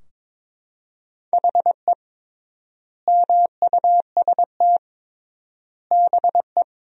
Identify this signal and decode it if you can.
Morse code, heard as a keyed tone, about 22 wpm, 700 Hz. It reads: HE MUST BE
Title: 3x Word-Spacing - Sentences from Top 500 Words - 22wpm